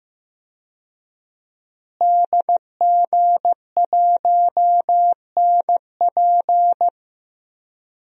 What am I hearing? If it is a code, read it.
Morse code, heard as a keyed tone, about 16 wpm, 700 Hz. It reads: DG1NP
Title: DXCC-100 Call Signs - 5 Characters - Repeated 2x - 15wpm